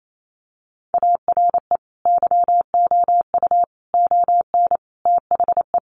Morse code, heard as a keyed tone, about 28 wpm, 700 Hz. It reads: ARE YOU ON THE
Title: Sentences from Top 1100 Words 28wpm